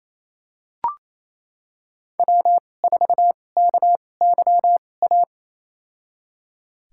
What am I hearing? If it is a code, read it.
Morse code, heard as a keyed tone, about 28 wpm, spaced as for 20 wpm, 700 Hz. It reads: W4KYA
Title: USA Call Signs - 5 Characters - 2x Character Spacing - 28wpm